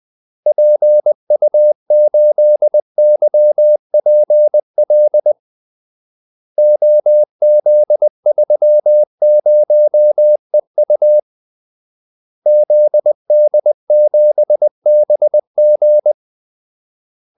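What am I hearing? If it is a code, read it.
Morse code, heard as a keyed tone, about 20 wpm, 600 Hz. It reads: PU8YPL OZ30EU ZD7BG
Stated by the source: Morse call-sign practice at 20 wpm.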